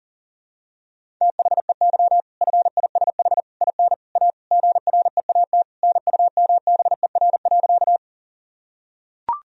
Morse code, heard as a keyed tone, about 37 wpm, 700 Hz. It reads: THEY FISH IN A GREAT NUMBER.
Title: Sentences from Top 1000 Words - Review Part 1 - 40wpm